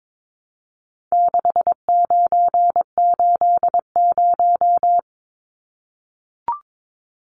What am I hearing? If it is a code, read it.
Morse code, heard as a keyed tone, about 22 wpm, 700 Hz. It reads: 6980
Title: Numbers - 4-Digit 22wpm